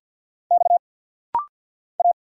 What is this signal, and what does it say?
Morse code, heard as a keyed tone, about 48 wpm, 700 Hz. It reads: X A
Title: Single Letters - Rapid-Fire 50wpm